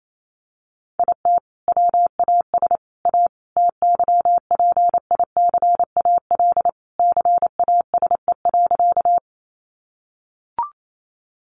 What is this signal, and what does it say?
Morse code, heard as a keyed tone, about 28 wpm, 700 Hz. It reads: IT WAS A TYPICAL CASE.
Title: Sentences from Top 1100 Words 28wpm